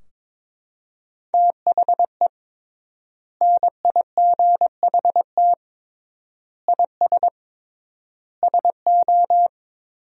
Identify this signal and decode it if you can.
Morse code, heard as a keyed tone, about 22 wpm, 700 Hz. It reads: THE NIGHT IS SO
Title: 3x Word-Spacing - Sentences from Top 500 Words - 22wpm